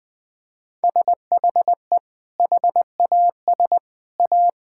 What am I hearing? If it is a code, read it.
Morse code, heard as a keyed tone, about 20 wpm, 700 Hz. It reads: SHE HAS A
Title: Sentences with 2-Word Phrases - Spoken Beginning - 20wpm